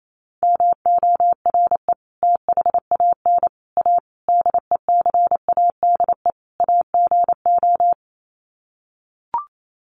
Morse code, heard as a keyed tone, about 28 wpm, 700 Hz. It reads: MORE THAN A DECADE AGO
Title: Sets of 5 Words Encore2 28wpm